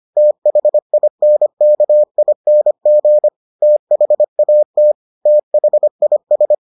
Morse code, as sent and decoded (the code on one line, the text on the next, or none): - .... .. -. -.- .. -. --. / - .... .- - / - .... .. ...
THINKING THAT THIS